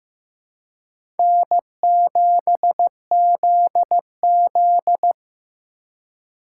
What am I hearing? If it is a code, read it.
Morse code, heard as a keyed tone, about 15 wpm, 700 Hz. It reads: N7ZZ
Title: USA Call Signs - 4 Characters - 4x Character Spacing - 15wpm